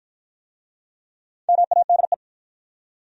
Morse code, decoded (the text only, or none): MADE